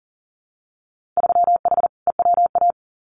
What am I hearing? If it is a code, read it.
Morse code, heard as a keyed tone, about 41 wpm, 700 Hz. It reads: 3H EWA
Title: Winter Field Day Exchanges Encore 40wpm